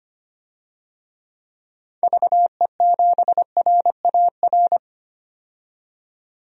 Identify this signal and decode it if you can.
Morse code, heard as a keyed tone, about 25 wpm, 700 Hz. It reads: VE7RAR
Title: DXCC-100 Call Signs - 6 Characters - 25wpm